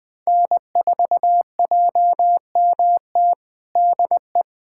Morse code, heard as a keyed tone, about 20 wpm, 700 Hz. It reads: N4JMT DE
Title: USA Callsign DE Callsign - 5 Characters - 20wpm